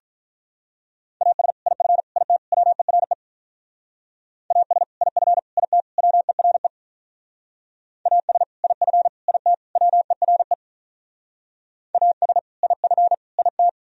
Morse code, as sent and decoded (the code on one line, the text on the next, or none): .- ... / .. ..-. / .. - / .-- . .-. / .- ... / .. ..-. / .. - / .-- . .-. / .- ... / .. ..-. / .. - / .-- . .-. / .- ... / .. ..-. / .. -
AS IF IT WERE AS IF IT WERE AS IF IT WERE AS IF IT